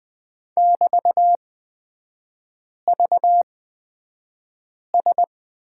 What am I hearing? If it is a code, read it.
Morse code, heard as a keyed tone, about 20 wpm, spaced as for 5 wpm, 700 Hz. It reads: <BT>VS
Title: 224 Lesson 36e - BT - Three Characters 20wpm